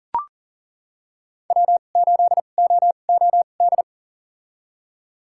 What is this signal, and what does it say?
Morse code, heard as a keyed tone, about 40 wpm, spaced as for 29 wpm, 700 Hz. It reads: W8OOD
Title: USA Call Signs - 5 Characters - 2x Character Spacing - 40wpm